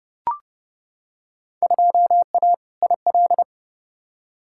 Morse code, heard as a keyed tone, about 30 wpm, 700 Hz. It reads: 2A IL
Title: Field Day Exchanges Encore - 30wpm